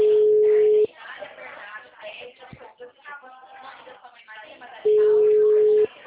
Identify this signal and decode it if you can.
Ringback tone (Europe, 425 Hz), with speech.